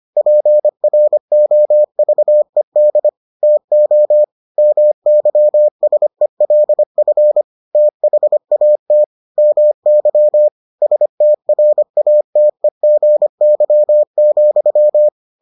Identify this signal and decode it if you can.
Morse code, heard as a keyed tone, about 25 wpm, 600 Hz. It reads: PROVED TO MYSELF THAT MY STRATEGY,